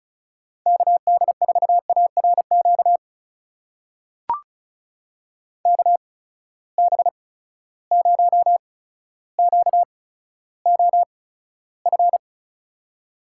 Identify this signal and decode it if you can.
Morse code, heard as a keyed tone, about 36 wpm, spaced as for 15 wpm, 700 Hz. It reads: KD4ARQ KB0QOF
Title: USA Call Signs - 6 Characters - 8x Character Spacing - 35wpm